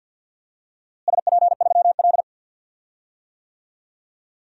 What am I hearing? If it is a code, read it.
Morse code, heard as a keyed tone, about 53 wpm, 700 Hz. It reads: SP3L
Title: DXCC-100 Call Signs - 4 Characters - Repeated 2x - 50wpm